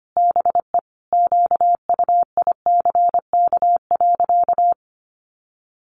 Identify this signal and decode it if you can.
Morse code, heard as a keyed tone, about 25 wpm, 700 Hz. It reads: BE QUICK.
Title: Sentences from Top 500 Words 25wpm